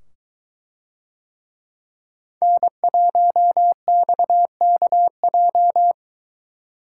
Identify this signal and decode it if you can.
Morse code, heard as a keyed tone, about 23 wpm, 700 Hz. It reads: N1XKJ